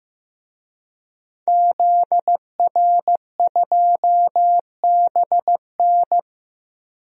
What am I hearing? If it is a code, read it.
Morse code, heard as a keyed tone, about 15 wpm, 700 Hz. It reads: ZR2BN
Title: DXCC-100 Call Signs - 5 Characters - Repeated 3x - 15wpm